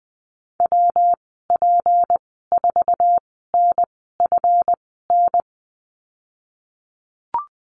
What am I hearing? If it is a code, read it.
Morse code, heard as a keyed tone, about 20 wpm, spaced as for 15 wpm, 700 Hz. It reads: WP4NFN
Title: 179 Lesson 29g - Slash - Call Signs 20wpm